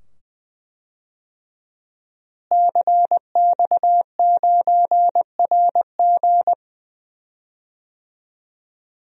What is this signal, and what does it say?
Morse code, heard as a keyed tone, about 20 wpm, 700 Hz. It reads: CX9RG
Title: DXCC-100 Call Signs - 5 Characters - Repeated 2x - 20wpm